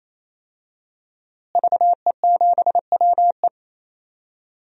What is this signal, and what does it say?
Morse code, heard as a keyed tone, about 28 wpm, 700 Hz. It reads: VE7WE
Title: DXCC-100 Call Signs - 5 Characters - Repeated 3x - 28wpm